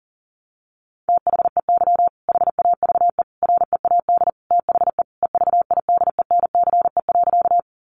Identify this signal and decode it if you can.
Morse code, heard as a keyed tone, about 40 wpm, 700 Hz. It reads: THEY HAVE READ THE EVIDENCE.
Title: Sentences from Top 1300 Words 40wpm